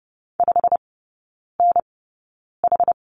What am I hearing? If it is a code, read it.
Morse code, heard as a keyed tone, about 30 wpm, spaced as for 9 wpm, 700 Hz. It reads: 5NH
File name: Lesson 16e - Letter c - Three Characters 30wpm